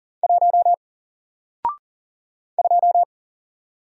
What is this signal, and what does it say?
Morse code, heard as a keyed tone, about 40 wpm, 700 Hz. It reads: 1 2
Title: Single Letter-Number 40wpm